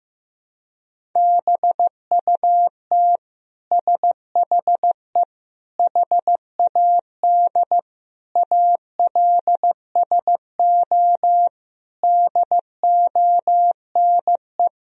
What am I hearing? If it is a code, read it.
Morse code, heard as a keyed tone, about 15 wpm, 700 Hz. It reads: BUT SHE HAD ALSO DONE
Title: Sentences from Top 1000 Words - Review Part 1 - 15wpm